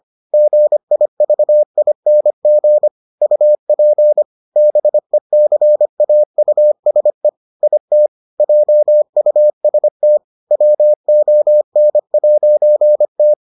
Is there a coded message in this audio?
Yes: Morse code from 0.3 to 13.3 s, heard as a keyed tone, about 25 wpm, 600 Hz. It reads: GIVING UP BECAUSE IT JUST WON'T